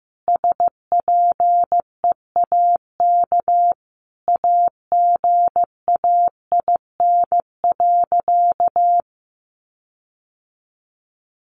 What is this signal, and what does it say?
Morse code, heard as a keyed tone, about 15 wpm, 700 Hz. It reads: SPEAK AGAIN.
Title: Sentences from Top 1500 Words 15wpm